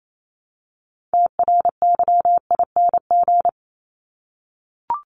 Morse code, heard as a keyed tone, about 28 wpm, 700 Hz. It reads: TRYING
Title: ING Suffix - Top 100 Words with Base Words 28wpm